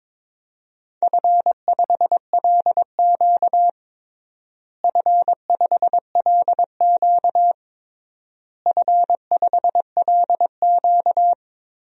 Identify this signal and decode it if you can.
Morse code, heard as a keyed tone, about 23 wpm, 700 Hz. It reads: F5LQ F5LQ F5LQ
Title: DXCC-100 Call Signs - 4 Characters - Repeated 3x - 22wpm